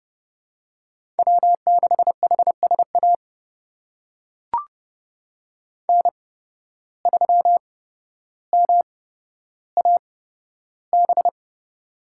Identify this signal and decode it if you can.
Morse code, heard as a keyed tone, about 30 wpm, spaced as for 13 wpm, 700 Hz. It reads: W6HSA N3MAB
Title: USA Call Signs - 5 Characters - 8x Character Spacing - 30wpm